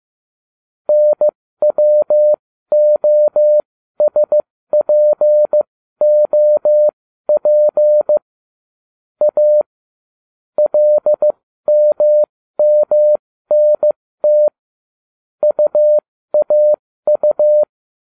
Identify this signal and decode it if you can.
Morse code, heard as a keyed tone, about 15 wpm, spaced as for 13 wpm, 600 Hz. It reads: NWOSPOP A LMMNT UAU